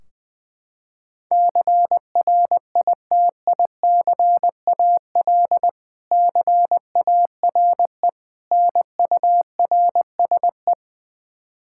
Morse code, heard as a keyed tone, about 20 wpm, 700 Hz. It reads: CRITICAL CARE NURSE
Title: K1USN SST QSO Practice 20wpm